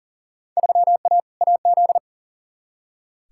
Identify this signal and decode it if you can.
Morse code, heard as a keyed tone, about 40 wpm, 700 Hz. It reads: 3A AZ
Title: Field Day Exchanges - 40wpm